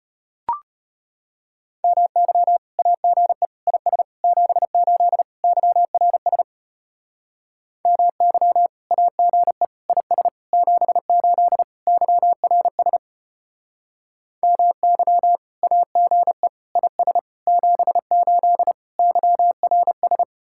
Morse code, heard as a keyed tone, about 33 wpm, 700 Hz. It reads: MY AGE IS 78 YRS MY AGE IS 78 YRS MY AGE IS 78 YRS